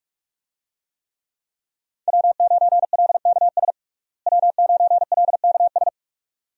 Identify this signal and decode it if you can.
Morse code, heard as a keyed tone, about 46 wpm, 700 Hz. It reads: W9LKS W9LKS
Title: USA Call Signs - 5 Characters - Repeated 2x - 45wpm